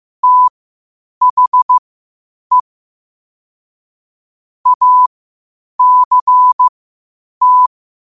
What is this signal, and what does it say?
Morse code, heard as a keyed tone, about 15 wpm, spaced as for 8 wpm, 1000 Hz. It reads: THE ACT
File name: qotdfiles / 8wpm